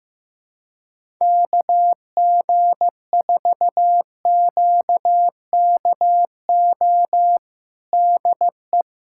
Morse code, heard as a keyed tone, about 15 wpm, 700 Hz. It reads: KG4QKO DE
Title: USA Callsign DE Callsign 15wpm